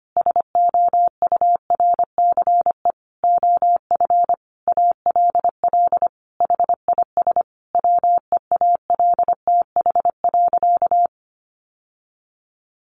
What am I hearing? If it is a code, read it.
Morse code, heard as a keyed tone, about 25 wpm, 700 Hz. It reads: SOURCE OF ALL HIS WEALTH.
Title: Sentences from Top 1200 Words 25wpm